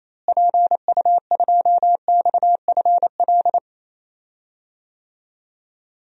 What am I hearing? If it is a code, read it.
Morse code, heard as a keyed tone, about 28 wpm, 700 Hz. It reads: PU2XFL
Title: DXCC-100 Call Signs - 6 Characters - Repeated 2x - 28wpm